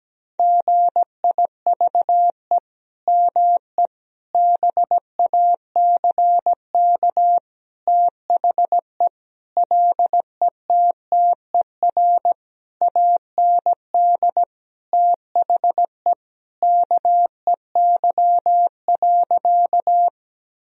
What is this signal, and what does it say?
Morse code, heard as a keyed tone, about 17 wpm, 700 Hz. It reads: GIVE ME BACK THE LETTER AND THE KEY.
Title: Sentences from Top 1000 Words - Review Part 4 - 17wpm